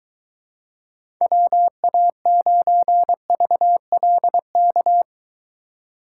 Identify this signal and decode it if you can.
Morse code, heard as a keyed tone, about 23 wpm, 700 Hz. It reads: WA9VLK